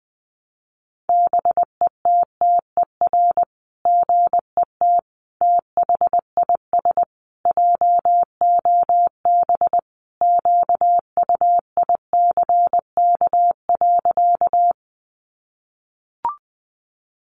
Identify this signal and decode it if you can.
Morse code, heard as a keyed tone, about 20 wpm, 700 Hz. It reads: BETTER GET THIS JOB QUICK.
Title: Sentences from Top 600 Words 20wpm